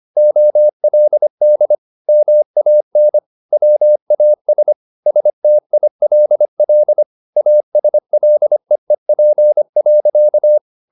Morse code, heard as a keyed tone, about 25 wpm, 600 Hz. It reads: OLD MAN WAS STILL ASLEEP.
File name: the old man and the sea part 1 25wpm